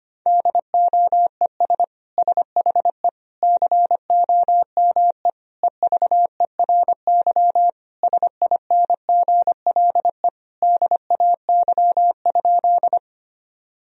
Morse code, heard as a keyed tone, about 25 wpm, 700 Hz. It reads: DOES SHE COME EVERY SINGLE DAY?
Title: Sentences from Top 1000 Words - Review Part 4 - 25wpm